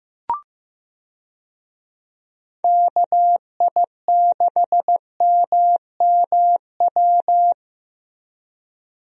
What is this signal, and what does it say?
Morse code, heard as a keyed tone, about 15 wpm, 700 Hz. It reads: KI6MMW
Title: USA Call Signs - 6 Characters - Repeated 2x - 15wpm